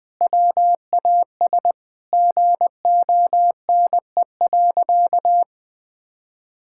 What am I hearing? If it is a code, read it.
Morse code, heard as a keyed tone, about 20 wpm, 700 Hz. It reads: WAS GONE.
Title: Sentences with 3-Word Phrases - Spoken Beginning - 20wpm